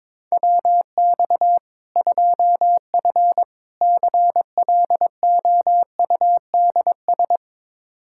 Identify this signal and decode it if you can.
Morse code, heard as a keyed tone, about 21 wpm, 700 Hz. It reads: WX 2F CLOUDS
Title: Practice QSO - Exchange 2 - Info - 22wpm